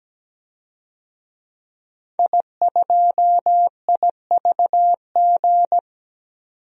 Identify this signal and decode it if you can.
Morse code, heard as a keyed tone, about 17 wpm, 700 Hz. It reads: I2IVG
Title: DXCC-100 Call Signs - 5 Characters - 17wpm